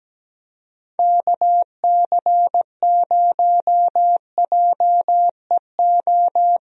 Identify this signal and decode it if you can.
Morse code, heard as a keyed tone, about 17 wpm, 700 Hz. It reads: KC0JEO
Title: USA Call Signs - 6 Characters - Repeated 3x - 17wpm